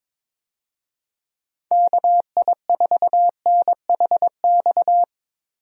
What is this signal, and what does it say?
Morse code, heard as a keyed tone, about 22 wpm, 700 Hz. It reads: KI4NHX
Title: USA Call Signs - 6 Characters - Repeated 2x - 22wpm